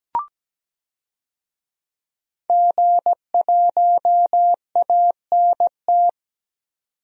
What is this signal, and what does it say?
Morse code, heard as a keyed tone, about 17 wpm, 700 Hz. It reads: G1ANT
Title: DXCC-100 Call Signs - 5 Characters - 17wpm